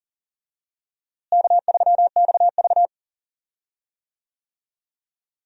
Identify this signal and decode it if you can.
Morse code, heard as a keyed tone, about 40 wpm, 700 Hz. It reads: K3XV
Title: USA Call Signs - 4 Characters - Repeated 2x - 40wpm